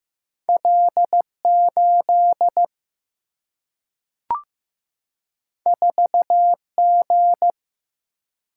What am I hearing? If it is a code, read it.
Morse code, heard as a keyed tone, about 15 wpm, 700 Hz. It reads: L8 4G
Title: Single-Letter Single-Number 15wpm